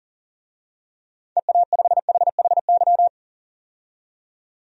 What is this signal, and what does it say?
Morse code, heard as a keyed tone, about 40 wpm, 700 Hz. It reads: EA5HHY
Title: DXCC-100 Call Signs - 6 Characters - 40wpm